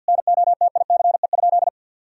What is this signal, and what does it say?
Morse code, heard as a keyed tone, about 50 wpm, 700 Hz. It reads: NOTICE?
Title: Sentences from Top 1000 Words - Review Part 2 - 50wpm